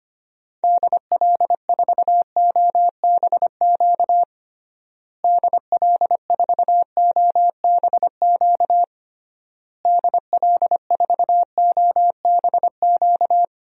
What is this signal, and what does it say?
Morse code, heard as a keyed tone, about 25 wpm, 700 Hz. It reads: DL4OBQ DL4OBQ DL4OBQ
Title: USA Call Signs - 6 Characters - Repeated 3x - 25wpm